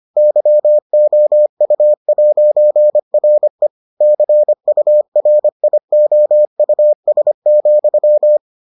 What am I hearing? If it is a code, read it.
Morse code, heard as a keyed tone, about 25 wpm, 600 Hz. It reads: YOU'RE CURIOUS,